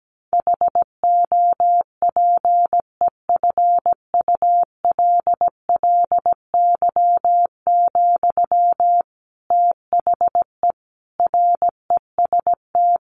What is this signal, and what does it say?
Morse code, heard as a keyed tone, about 17 wpm, 700 Hz. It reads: HOPEFULLY, THE REST
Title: Sentences from Top 1500 Words 17wpm